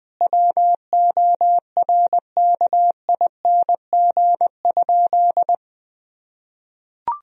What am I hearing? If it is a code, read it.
Morse code, heard as a keyed tone, about 20 wpm, 700 Hz. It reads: WORKING?
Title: Sentences from Top 1000 Words - Review Part 4 - 20wpm